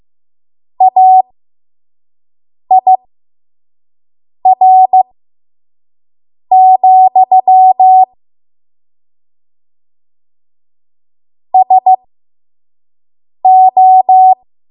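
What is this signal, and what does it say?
Morse code, heard as a keyed tone, about 15 wpm, spaced as for 5 wpm, 750 Hz. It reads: AIR, SO
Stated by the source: 5 WPM Morse code practice.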